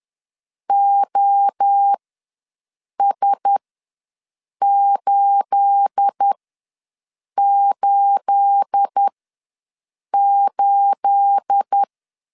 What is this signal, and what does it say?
Morse code, heard as a keyed tone, about 10 wpm, spaced as for 6 wpm, 800 Hz. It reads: OS888
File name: LEZIONI / 9 28 ws 10wpm 4124